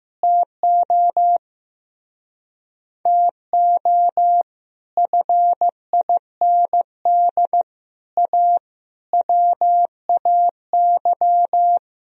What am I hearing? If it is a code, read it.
Morse code, heard as a keyed tone, about 15 wpm, 700 Hz. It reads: TO TO FIND A WAY
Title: Speed-Racing - Sets of 5 Words 15wpm